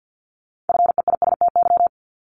Morse code, heard as a keyed tone, about 50 wpm, 700 Hz. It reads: FEISTY
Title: Multi-letter Phonemes 50wpm